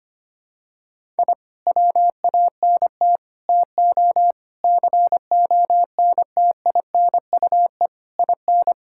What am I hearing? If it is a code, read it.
Morse code, heard as a keyed tone, about 25 wpm, 700 Hz. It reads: I WANT TO CONTINUE IN
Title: Sentences from Top 1000 Words - Review Part 6 - 25wpm